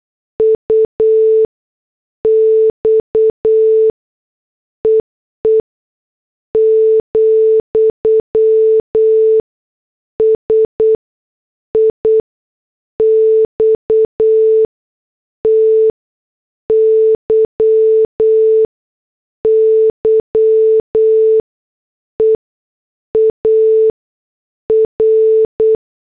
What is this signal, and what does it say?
Morse code, heard as a keyed tone, about 8 wpm, spaced as for 6 wpm, 440 Hz. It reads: UX EE, SIXTY YEAR